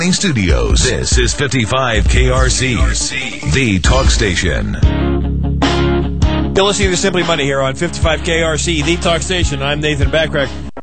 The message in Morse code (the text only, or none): none